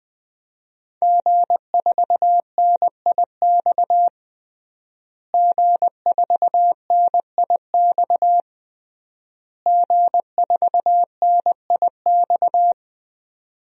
Morse code, heard as a keyed tone, about 21 wpm, 700 Hz. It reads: G4NIX G4NIX G4NIX
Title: DXCC-100 Call Signs - 5 Characters - Repeated 3x - 20wpm